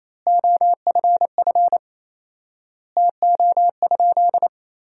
Morse code, heard as a keyed tone, about 28 wpm, 700 Hz. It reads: OFF TO?
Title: 4x Word-Spacing - Sentences from Top 300 Words - 28wpm